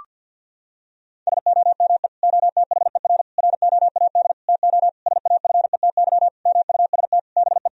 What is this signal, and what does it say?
Morse code, heard as a keyed tone, about 50 wpm, 700 Hz. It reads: SOME OTHER ROAD TO SAFETY MUST BE
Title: Sentences from Top 1000 Words - Review Part 4 - 50wpm